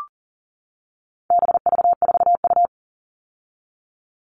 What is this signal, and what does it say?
Morse code, heard as a keyed tone, about 40 wpm, 700 Hz. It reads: BV4U
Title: All Cut Numbers - 4-Digit 40wpm